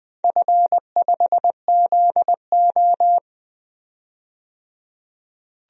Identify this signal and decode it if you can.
Morse code, heard as a keyed tone, about 20 wpm, 700 Hz. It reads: F5ZO
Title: DXCC-100 Call Signs - 4 Characters - 20wpm